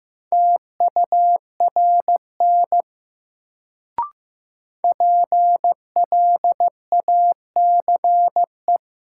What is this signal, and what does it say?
Morse code, heard as a keyed tone, about 15 wpm, 700 Hz. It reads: TURN PLACE
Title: Sentences from Top 200 Words - New Word Drill - 15wpm